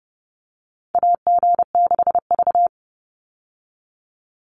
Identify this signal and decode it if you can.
Morse code, heard as a keyed tone, about 30 wpm, 700 Hz. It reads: AG6V
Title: Call Signs - 30wpm